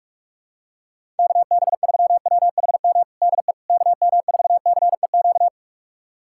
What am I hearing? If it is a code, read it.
Morse code, heard as a keyed tone, about 46 wpm, 700 Hz. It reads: KB3WHM DE KM4CEQ
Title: USA Callsign DE Callsign - 6 Characters - 45wpm